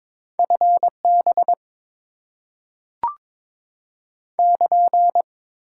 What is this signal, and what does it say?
Morse code, heard as a keyed tone, about 22 wpm, 700 Hz. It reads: FB <KN>
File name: Top 90 QSO Elements - 6x Character Spacing - 22wpm